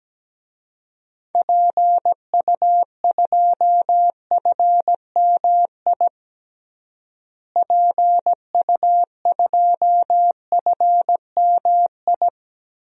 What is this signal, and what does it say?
Morse code, heard as a keyed tone, about 17 wpm, 700 Hz. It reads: PU2FMI PU2FMI
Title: DXCC-100 Call Signs - 6 Characters - Repeated 3x - 17wpm